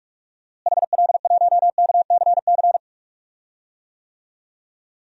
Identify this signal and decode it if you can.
Morse code, heard as a keyed tone, about 46 wpm, 700 Hz. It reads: HL1KCC
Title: DXCC-100 Call Signs - 6 Characters - Repeated 3x - 45wpm